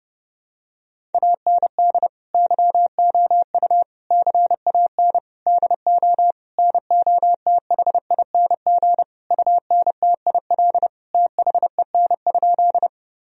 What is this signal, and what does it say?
Morse code, heard as a keyed tone, about 32 wpm, 700 Hz. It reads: AND YOU CAN DO NOTHING UNTIL THEN?
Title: Sentences from Top 1000 Words - Review Part 1 - 30wpm